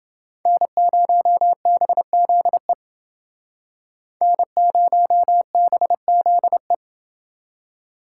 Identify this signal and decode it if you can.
Morse code, heard as a keyed tone, about 29 wpm, 700 Hz. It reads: N0BZE N0BZE